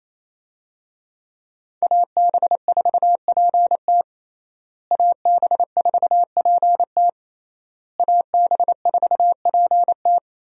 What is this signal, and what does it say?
Morse code, heard as a keyed tone, about 28 wpm, 700 Hz. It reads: AB4PT AB4PT AB4PT
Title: USA Call Signs - 5 Characters - Repeated 3x - 28wpm